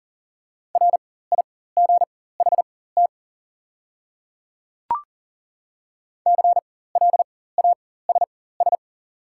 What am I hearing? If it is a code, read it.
Morse code, heard as a keyed tone, about 40 wpm, spaced as for 19 wpm, 690 Hz. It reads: RIGHT CLASS
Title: 4x Character Spacing - Top 500 Words - 40wpm